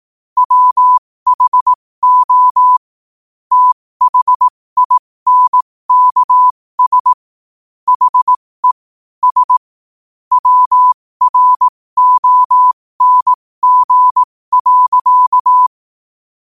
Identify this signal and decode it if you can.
Morse code, heard as a keyed tone, about 18 wpm, spaced as for 15 wpm, 1000 Hz. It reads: WHO THINKS HE S WRONG.
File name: qotdfiles / 15wpm